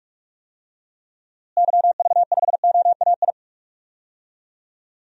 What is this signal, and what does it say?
Morse code, heard as a keyed tone, about 46 wpm, 700 Hz. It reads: YV5OAI
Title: DXCC-100 Call Signs - 6 Characters - Repeated 3x - 45wpm